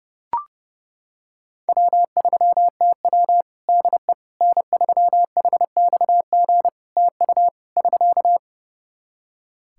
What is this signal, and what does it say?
Morse code, heard as a keyed tone, about 32 wpm, 700 Hz. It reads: W3TW DE N3HXG TU <SK>
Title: Practice QSO - Exchange 4 - Ending - 30wpm